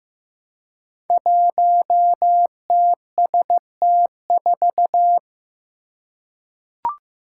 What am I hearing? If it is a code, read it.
Morse code, heard as a keyed tone, about 15 wpm, 700 Hz. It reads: 1TST4
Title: Five Challenging Letter-Number Combinations 15wpm